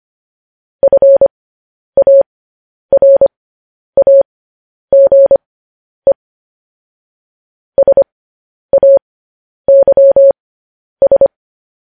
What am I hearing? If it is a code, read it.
Morse code, heard as a keyed tone, about 25 wpm, spaced as for 10 wpm, 550 Hz. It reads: FARAGE SAYS
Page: news-headlines-in-morse-code-at-10-wpm